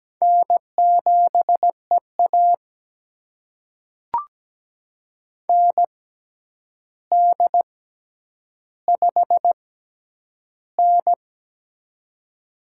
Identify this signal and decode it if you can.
Morse code, heard as a keyed tone, about 17 wpm, spaced as for 9 wpm, 700 Hz. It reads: N7EA ND5N
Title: USA Call Signs - 4 Characters - 6x Character Spacing - 17wpm